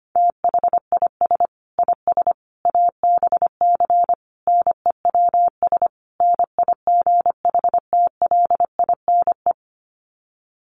Morse code, heard as a keyed tone, about 25 wpm, 700 Hz. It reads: THIS IS ABC NEWS NIGHTLINE
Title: Sets of 5 Words 25wpm